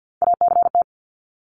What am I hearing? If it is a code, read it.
Morse code, heard as a keyed tone, about 50 wpm, 700 Hz. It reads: ACT
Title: Top 300 Words Repeated 50wpm